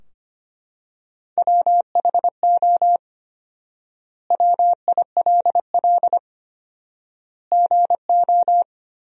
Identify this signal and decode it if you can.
Morse code, heard as a keyed tone, about 25 wpm, 700 Hz. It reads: WHO WILL GO
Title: 4x Word-Spacing - Sentences from Top 100 Words - 25wpm